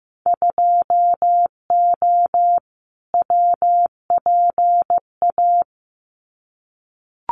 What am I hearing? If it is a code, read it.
Morse code, heard as a keyed tone, about 15 wpm, 700 Hz. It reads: 2O WPA
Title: Winter Field Day Exchanges 15wpm